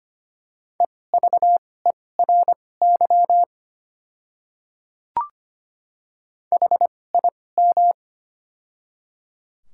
Morse code, heard as a keyed tone, about 25 wpm, spaced as for 18 wpm, 690 Hz. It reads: EVERY HIM
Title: 2x Character Spacing - Top 200 Words - 25wpm